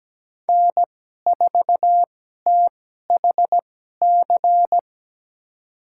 Morse code, heard as a keyed tone, about 17 wpm, spaced as for 12 wpm, 700 Hz. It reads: N4THC
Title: USA Call Signs - 5 Characters - 2x Character Spacing - 17wpm